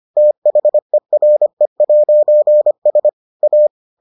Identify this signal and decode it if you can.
Morse code, heard as a keyed tone, about 25 wpm, 600 Hz. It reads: THERE'S A